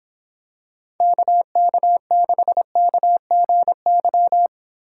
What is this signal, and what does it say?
Morse code, heard as a keyed tone, about 26 wpm, 700 Hz. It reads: KK6KGY